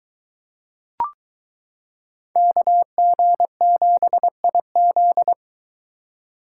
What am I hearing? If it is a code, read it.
Morse code, heard as a keyed tone, about 23 wpm, 700 Hz. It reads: KG7IZ